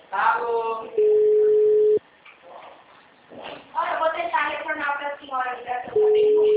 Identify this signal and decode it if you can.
Ringback tone (Europe, 425 Hz), with speech.